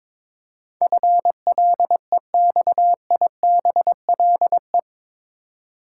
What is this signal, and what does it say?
Morse code, heard as a keyed tone, about 22 wpm, 700 Hz. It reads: FLEXIBLE
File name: practice / 22wpm / Eight Letter Words - 22wpm